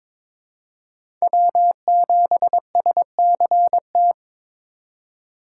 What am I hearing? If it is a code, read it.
Morse code, heard as a keyed tone, about 22 wpm, 700 Hz. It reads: W7SCT
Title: USA Call Signs - 3x Character Spacing - 22wpm